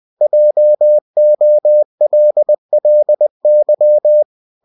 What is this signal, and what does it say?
Morse code, heard as a keyed tone, about 20 wpm, 600 Hz. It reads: JOLLY
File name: cw-books / winnie-the-pooh / winnie-20wpm-page-0091